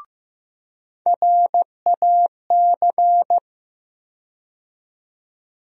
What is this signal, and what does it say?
Morse code, heard as a keyed tone, about 15 wpm, 700 Hz. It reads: RAC
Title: USA Call Sign Postfixes - 3 Letters 15wpm